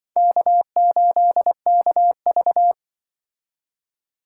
Morse code, heard as a keyed tone, about 25 wpm, 700 Hz. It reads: K8KV
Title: Speed Racing - USA 1x2 Call Signs 20wpm